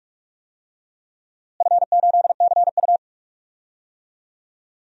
Morse code, heard as a keyed tone, about 46 wpm, 700 Hz. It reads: F8CU
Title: DXCC-100 Call Signs - 4 Characters - Repeated 2x - 45wpm